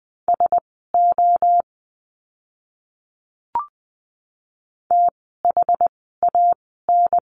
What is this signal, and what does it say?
Morse code, heard as a keyed tone, about 20 wpm, spaced as for 14 wpm, 700 Hz. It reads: SO THAN
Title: 108 Lesson 18f - Letter m - Words 20wpm